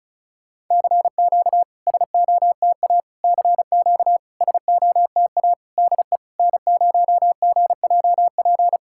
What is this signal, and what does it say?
Morse code, heard as a keyed tone, about 34 wpm, 690 Hz. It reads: CQ SOTA CQ SOTA DE N0GJP
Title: SOTA QSO Practice 35wpm